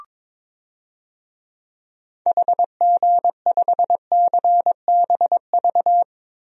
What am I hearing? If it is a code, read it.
Morse code, heard as a keyed tone, about 22 wpm, 700 Hz. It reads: HG5CBV
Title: DXCC-100 Call Signs - 6 Characters - Repeated 2x - 22wpm